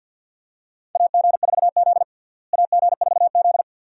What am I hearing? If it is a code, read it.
Morse code, heard as a keyed tone, about 52 wpm, 700 Hz. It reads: AG4Z AG4Z
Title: USA Call Signs - 4 Characters - Repeated 2x - 50wpm